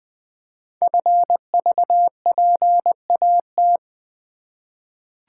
Morse code, heard as a keyed tone, about 20 wpm, 700 Hz. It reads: FVPAT
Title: Code Groups - 5 Letters - 20wpm